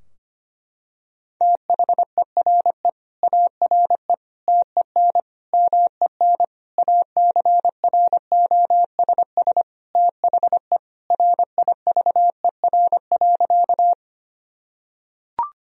Morse code, heard as a keyed tone, about 25 wpm, 700 Hz. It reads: THERE ARE TEN MEN ACROSS THE RIVER.
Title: Sentences from Top 1000 Words - Review Part 2 - 25wpm